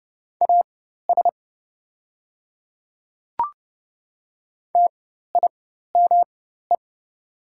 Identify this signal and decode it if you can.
Morse code, heard as a keyed tone, about 30 wpm, spaced as for 14 wpm, 700 Hz. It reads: AS TIME
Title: 4x Character Spacing - Top 200 Words - 30wpm